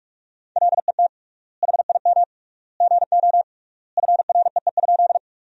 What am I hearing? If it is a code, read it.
Morse code, heard as a keyed tone, about 46 wpm, 700 Hz. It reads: LET HIM GO FREE?
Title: 3x Word-Spacing - Sentences from Top 500 Words - 45wpm